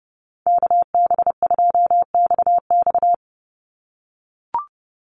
Morse code, heard as a keyed tone, about 30 wpm, 700 Hz. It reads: KB2XX
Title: NA SOTA Chaser Callsigns 30wpm